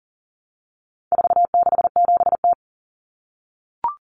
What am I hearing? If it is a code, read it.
Morse code, heard as a keyed tone, about 40 wpm, 700 Hz. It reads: 467T
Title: Common Cut Numbers - 4-Digit 40wpm